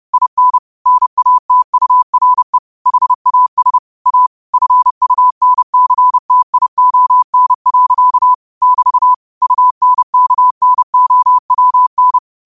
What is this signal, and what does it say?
Morse code, heard as a keyed tone, about 30 wpm, 1000 Hz. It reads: IN NATURE HAS A FUNCTION. <BT> UNKNOWN
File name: qotdfiles / 30wpm